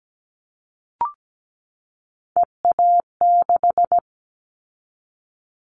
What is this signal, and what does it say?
Morse code, heard as a keyed tone, about 17 wpm, 700 Hz. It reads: EA6
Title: Callsign Prefixes for Top 100 DXCC - 17wpm